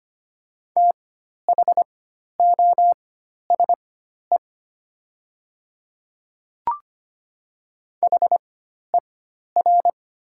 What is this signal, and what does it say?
Morse code, heard as a keyed tone, about 25 wpm, spaced as for 12 wpm, 700 Hz. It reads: THOSE HER
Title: 4x Character Spacing - Top 500 Words - 25wpm